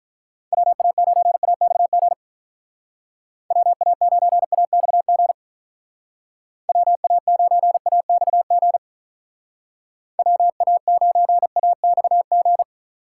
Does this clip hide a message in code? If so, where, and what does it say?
Yes: Morse code from 0.5 to 12.6 s, heard as a keyed tone, about 44 wpm, 700 Hz. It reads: WA9AXG WA9AXG WA9AXG WA9AXG